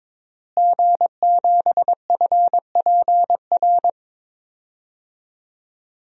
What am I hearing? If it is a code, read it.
Morse code, heard as a keyed tone, about 22 wpm, 700 Hz. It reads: G7FPR
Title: DXCC-100 Call Signs - 5 Characters - Repeated 2x - 22wpm